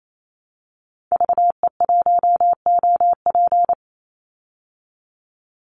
Morse code, heard as a keyed tone, about 28 wpm, 690 Hz. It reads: VE1OP
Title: Call Signs Repeated 3x - 28wpm